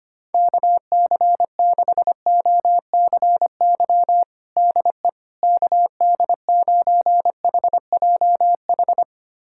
Morse code, heard as a keyed tone, about 25 wpm, 700 Hz. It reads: KC6OCY DE KD9HJH
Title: USA Callsign DE Callsign - 6 Characters - 25wpm